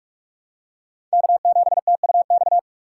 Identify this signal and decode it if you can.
Morse code, heard as a keyed tone, about 47 wpm, 700 Hz. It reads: K7TUX
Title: USA Call Signs - 5 Characters - 3x Character Spacing - 45wpm